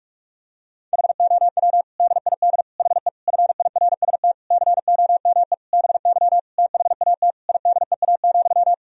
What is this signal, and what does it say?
Morse code, heard as a keyed tone, about 47 wpm, 700 Hz. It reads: HOW DID HE FIRST COME BY THAT IDEA,
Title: Sentences from Top 1000 Words - Review Part 2 - 45wpm